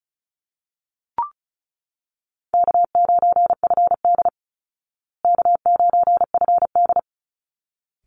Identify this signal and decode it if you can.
Morse code, heard as a keyed tone, about 34 wpm, 700 Hz. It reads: K9FD K9FD
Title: Easy Call Signs Repeated 2x - 35wpm